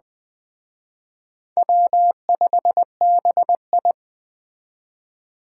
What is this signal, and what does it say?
Morse code, heard as a keyed tone, about 20 wpm, 700 Hz. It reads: W5BI
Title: Speed Racing - USA Call Signs 17wpm